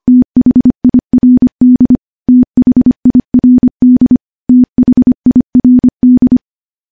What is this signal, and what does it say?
Morse code, heard as a keyed tone, about 25 wpm, 250 Hz. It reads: THIRD THIRD THIRD
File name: morse / morse-training-25WPM